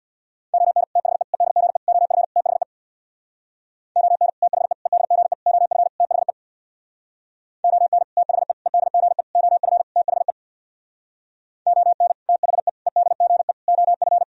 Morse code, heard as a keyed tone, about 64 wpm, 700 Hz. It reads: ON THE EDGE OF THE ON THE EDGE OF THE ON THE EDGE OF THE ON THE EDGE OF